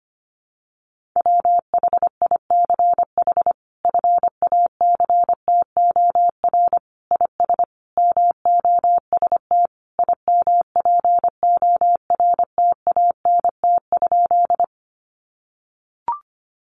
Morse code, heard as a keyed tone, about 25 wpm, 700 Hz. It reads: WHICH FACTOR IS MOST IMPORTANT?
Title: Sentences from Top 1300 Words 25wpm